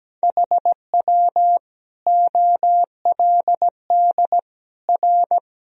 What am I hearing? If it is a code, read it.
Morse code, heard as a keyed tone, about 17 wpm, 700 Hz. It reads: HW OLD R
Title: Sentences from Top 200 Words and Abbreviations 17wpm